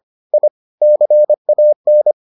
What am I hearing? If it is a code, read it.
Morse code, heard as a keyed tone, about 25 wpm, 600 Hz. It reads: I CAN